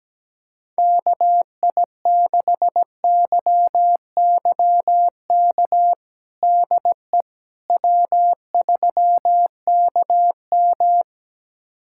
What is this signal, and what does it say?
Morse code, heard as a keyed tone, about 17 wpm, 700 Hz. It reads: KI6YYK DE W3KM
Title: USA Callsign DE Callsign 17wpm